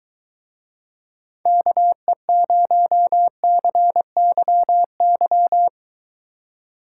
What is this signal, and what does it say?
Morse code, heard as a keyed tone, about 23 wpm, 700 Hz. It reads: KE0CYY